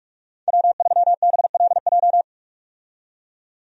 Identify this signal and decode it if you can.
Morse code, heard as a keyed tone, about 46 wpm, 700 Hz. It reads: W3BLJ
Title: USA Call Signs - 5 Characters - 45wpm